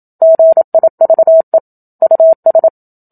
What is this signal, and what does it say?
Morse code, heard as a keyed tone, about 27 wpm, 650 Hz. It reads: GIVE US
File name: Book Two / 01-Chapter 1 - 27 WPM